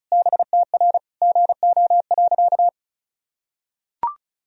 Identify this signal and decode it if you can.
Morse code, heard as a keyed tone, about 34 wpm, 700 Hz. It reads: BTR GO.
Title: Sentences from Top 400 Words and Abbreviations 35wpm